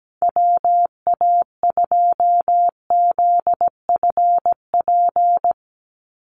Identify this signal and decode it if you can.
Morse code, heard as a keyed tone, about 17 wpm, 700 Hz. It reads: WA2ZFP
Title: USA Call Signs 17wpm